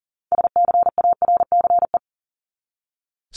Morse code, heard as a keyed tone, about 40 wpm, 700 Hz. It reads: SCARCE
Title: LY Suffix - Top 100 Words with Base Words 40wpm